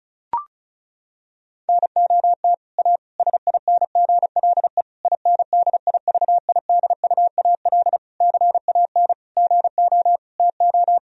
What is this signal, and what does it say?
Morse code, heard as a keyed tone, about 36 wpm, 700 Hz. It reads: NOT A SINGLE INDIVIDUAL CAN GO TO
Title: Sentences from Top 1000 Words - Review Part 4 - 35wpm